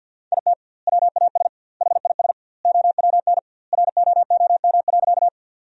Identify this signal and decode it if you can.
Morse code, heard as a keyed tone, about 50 wpm, 700 Hz. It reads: IT WAS HIS OWN ROOM.
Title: 2x Word-Spacing - Sentences from Top 300 Words - 50wpm